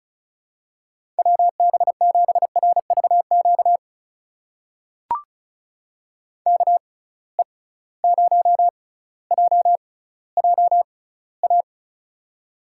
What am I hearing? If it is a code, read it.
Morse code, heard as a keyed tone, about 36 wpm, spaced as for 18 wpm, 700 Hz. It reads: WB7RVQ KE0JJA
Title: USA Call Signs - 6 Characters - 6x Character Spacing - 35wpm